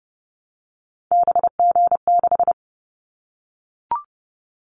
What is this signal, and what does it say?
Morse code, heard as a keyed tone, about 30 wpm, 700 Hz. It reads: BG6
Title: Callsign Prefixes for Top 100 DXCC - 30wpm